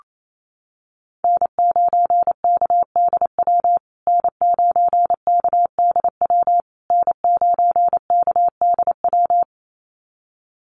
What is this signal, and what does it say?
Morse code, heard as a keyed tone, about 28 wpm, 700 Hz. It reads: N9KDW N9KDW N9KDW
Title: USA Call Signs Repeated 3x 28wpm